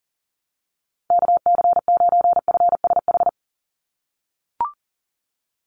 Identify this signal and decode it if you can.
Morse code, heard as a keyed tone, about 40 wpm, 700 Hz. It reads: KC9FSH
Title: USA Call Signs 40wpm